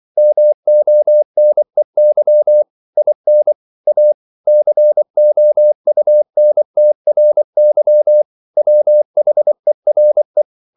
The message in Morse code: MONEY IN A COUNTRY WHERE

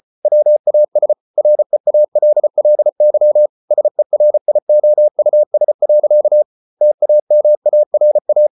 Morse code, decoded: WAS REALLY SERIOUS. TAMARA

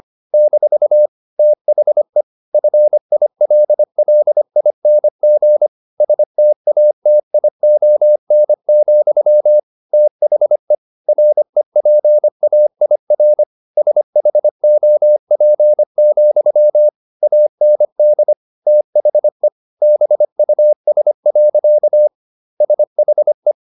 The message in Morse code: - THE FILLING STATION, THE REPAIR SHOP, AND THE BUS. SHE

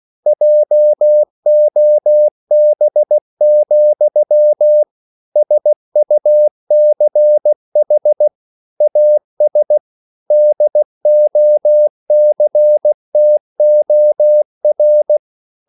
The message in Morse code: JOB, SUCH AS DOCTOR

